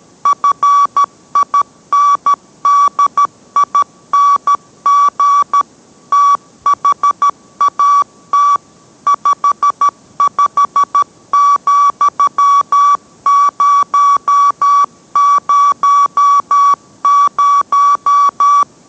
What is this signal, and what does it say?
Morse code, heard as a keyed tone, about 13 wpm, 1140 Hz. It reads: FINDING THAT 55,000